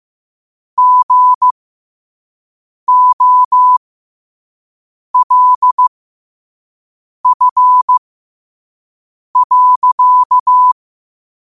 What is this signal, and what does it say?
Morse code, heard as a keyed tone, about 15 wpm, spaced as for 5 wpm, 1000 Hz. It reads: GOLF.
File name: qotdfiles / 5wpm